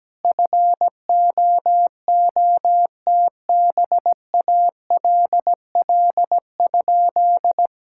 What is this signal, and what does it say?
Morse code, heard as a keyed tone, about 17 wpm, 700 Hz. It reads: FOOTBALL?
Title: Sentences from Top 1600 Words 17wpm